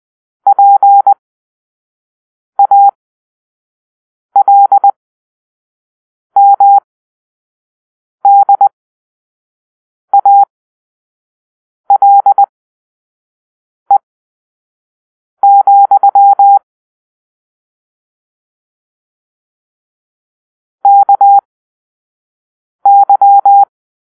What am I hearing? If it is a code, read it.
Morse code, heard as a keyed tone, about 20 wpm, spaced as for 5 wpm, 800 Hz. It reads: PALMDALE, KY